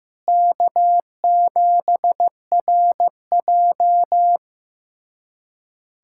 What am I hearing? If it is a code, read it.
Morse code, heard as a keyed tone, about 16 wpm, 700 Hz. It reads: K7RJ
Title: USA Callsign DE Callsign - 4 Characters - 15wpm